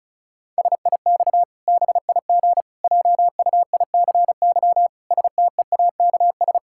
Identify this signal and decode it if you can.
Morse code, heard as a keyed tone, about 36 wpm, 700 Hz. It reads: SIX BIG JUICY STEAKS
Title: Pangram Challenge v2 - 35wpm